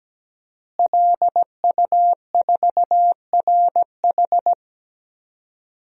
Morse code, heard as a keyed tone, about 17 wpm, 700 Hz. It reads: LU4RH
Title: DXCC-100 Call Signs - 5 Characters - Repeated 2x - 17wpm